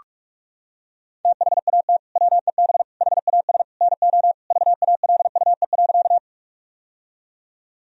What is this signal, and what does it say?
Morse code, heard as a keyed tone, about 41 wpm, 700 Hz. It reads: THAT WEB HAS NO VALUE.
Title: Sentences from Top 1000 Words - Review Part 4 - 45wpm